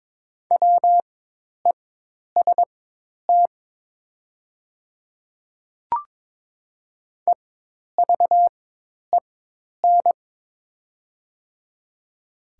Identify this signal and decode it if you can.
Morse code, heard as a keyed tone, about 22 wpm, spaced as for 10 wpm, 700 Hz. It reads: WEST EVEN
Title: 4x Character Spacing - Top 500 Words - 22wpm